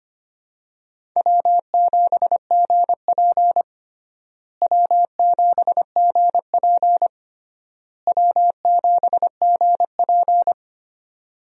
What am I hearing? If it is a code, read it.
Morse code, heard as a keyed tone, about 25 wpm, 700 Hz. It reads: W7GP W7GP W7GP
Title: USA Call Signs - 4 Characters - Repeated 3x - 25wpm